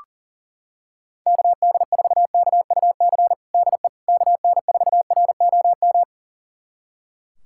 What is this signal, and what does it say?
Morse code, heard as a keyed tone, about 37 wpm, 700 Hz. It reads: KD4KUC DE KN4ROM
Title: USA Callsign DE Callsign 40wpm